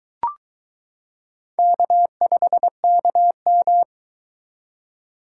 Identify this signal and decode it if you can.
Morse code, heard as a keyed tone, about 22 wpm, 700 Hz. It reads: K5KM